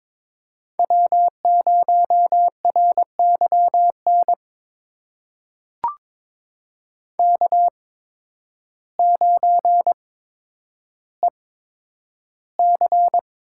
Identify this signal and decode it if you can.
Morse code, heard as a keyed tone, about 22 wpm, spaced as for 10 wpm, 700 Hz. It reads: W0RYN K9EC
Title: USA Call Signs - 8x Character Spacing - 22wpm